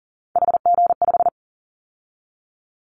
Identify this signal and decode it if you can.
Morse code, heard as a keyed tone, about 40 wpm, 700 Hz. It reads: HG5
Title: Callsign Prefixes for Top 25 DXCC - 40wpm